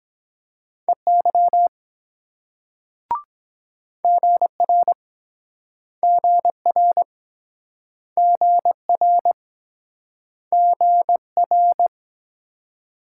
Morse code, heard as a keyed tone, about 24 wpm, 700 Hz. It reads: EY GR GR GR GR